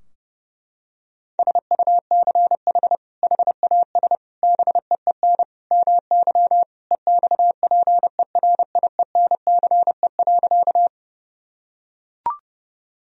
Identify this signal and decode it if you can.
Morse code, heard as a keyed tone, about 30 wpm, 700 Hz. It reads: SUCH HAS BEEN MY EXPERIENCE.
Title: Sentences from Top 1000 Words - Review Part 3 - 30wpm